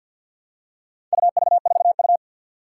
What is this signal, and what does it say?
Morse code, heard as a keyed tone, about 53 wpm, 700 Hz. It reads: UV4U
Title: Four Challenging Letter-Number Combinations 50wpm